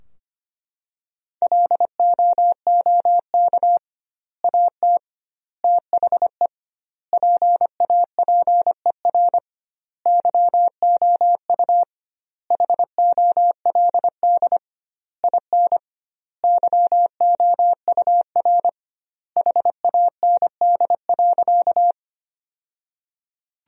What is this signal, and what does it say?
Morse code, heard as a keyed tone, about 25 wpm, 700 Hz. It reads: LOOK AT THE PAPER YOU HOLD IN YOUR HAND.
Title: 2x Word-Spacing - Sentences from Top 400 Words - 25wpm